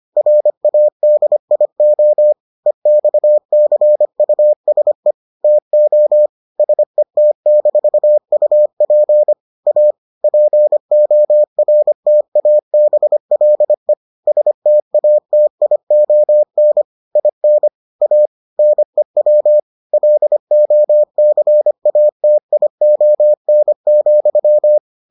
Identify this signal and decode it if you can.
Morse code, heard as a keyed tone, about 25 wpm, 600 Hz. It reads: RADIO EXCUSE TO SET-UP A PORTABLE STATION IN A NEW LOCATION,